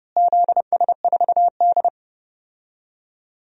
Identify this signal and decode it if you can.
Morse code, heard as a keyed tone, about 30 wpm, 700 Hz. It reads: ZS4D
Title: DXCC-100 Call Signs - 4 Characters - Repeated 3x - 30wpm